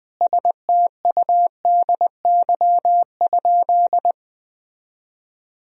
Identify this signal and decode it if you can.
Morse code, heard as a keyed tone, about 21 wpm, 700 Hz. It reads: STUDY?
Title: Sentences from Top 1000 Words - Review Part 2 - 20wpm